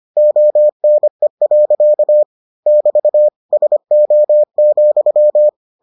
Morse code, heard as a keyed tone, about 25 wpm, 600 Hz. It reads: ONE. <BT> SO,